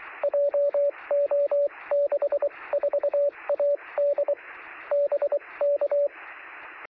Morse code, heard as a keyed tone, about 24 wpm, spaced as for 20 wpm, 570 Hz, with radio noise. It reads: JO64AD BK
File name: sm7lcb / 1856z dl0vv 13cm